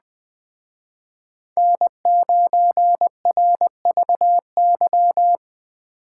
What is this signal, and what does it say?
Morse code, heard as a keyed tone, about 22 wpm, 700 Hz. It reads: N9RVY